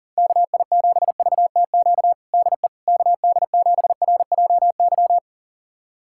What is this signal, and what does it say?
Morse code, heard as a keyed tone, about 37 wpm, 700 Hz. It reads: KI7VTQ DE KD7RJY
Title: USA Callsign DE Callsign - 6 Characters - 40wpm